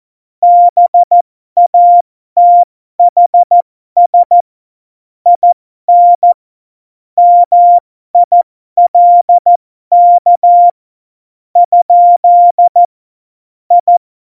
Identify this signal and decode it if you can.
Morse code, heard as a keyed tone, about 14 wpm, spaced as for 11 wpm, 700 Hz. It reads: BATHS IN MILK? I
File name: Morsekurs / Witze